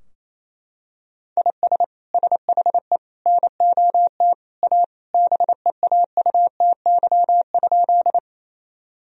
Morse code, heard as a keyed tone, about 27 wpm, 700 Hz. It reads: IS SHE NOT A BEAUTY?
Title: Sentences from Top 1000 Words - Review Part 2 - 28wpm